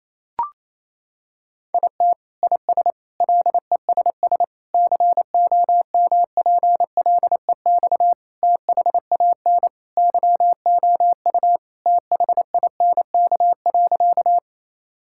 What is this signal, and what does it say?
Morse code, heard as a keyed tone, about 28 wpm, 700 Hz. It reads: IT IS LESS COMPLEX THAN YOU THINK.
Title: Sentences from Top 1000 Words - Review Part 6 - 28wpm